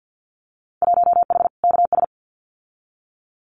Morse code, heard as a keyed tone, about 50 wpm, 700 Hz. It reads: 1H KS
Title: Winter Field Day Exchanges 50wpm